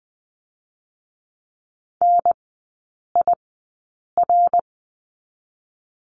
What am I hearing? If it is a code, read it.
Morse code, heard as a keyed tone, about 20 wpm, spaced as for 8 wpm, 700 Hz. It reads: NIR
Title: 087 Lesson 15e - Number 5 - Three Characters 20wpm